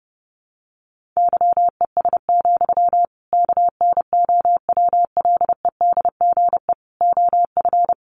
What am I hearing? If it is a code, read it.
Morse code, heard as a keyed tone, about 30 wpm, 700 Hz. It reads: YES, KNOWLEDGE OF
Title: Sentences from Top 1500 Words 30wpm